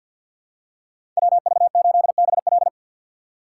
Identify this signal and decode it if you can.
Morse code, heard as a keyed tone, about 52 wpm, 700 Hz. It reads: WV8BL
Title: USA Call Signs - 4x Character Spacing - 50wpm